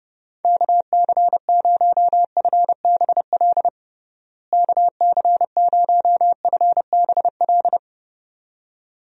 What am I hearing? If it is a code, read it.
Morse code, heard as a keyed tone, about 30 wpm, 700 Hz. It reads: KC0FBL KC0FBL
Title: USA Call Signs - 6 Characters - Repeated 2x - 30wpm